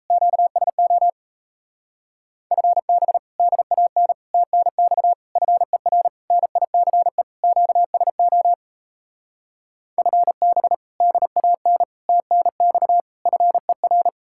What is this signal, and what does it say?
Morse code, heard as a keyed tone, about 32 wpm, 700 Hz. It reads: QSO FB DAN TNX FER NICE QSO FB DAN TNX FER
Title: Speed-Racing Common QSO Phrases Encore - 28wpm